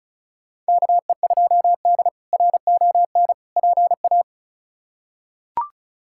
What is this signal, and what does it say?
Morse code, heard as a keyed tone, about 36 wpm, 700 Hz. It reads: KE2D RON PA
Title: K1USN SST QSO Practice - 35wpm